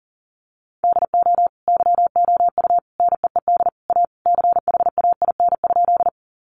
Morse code, heard as a keyed tone, about 40 wpm, 700 Hz. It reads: DO YOU NEED A CHAIN?
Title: Sentences from Top 1500 Words 40wpm